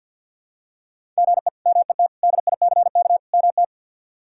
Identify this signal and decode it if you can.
Morse code, heard as a keyed tone, about 53 wpm, 700 Hz. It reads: GE MET DICK MT
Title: K1USN SST QSO Practice - 50wpm